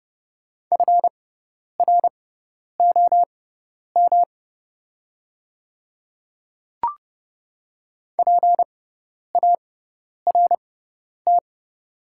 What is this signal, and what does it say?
Morse code, heard as a keyed tone, about 30 wpm, spaced as for 10 wpm, 700 Hz. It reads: FROM PART
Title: 6x Character Spacing - Top 200 Words - 30wpm